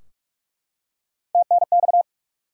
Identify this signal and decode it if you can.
Morse code, heard as a keyed tone, about 44 wpm, 700 Hz. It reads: TNX
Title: Top 90 QSO Elements 45wpm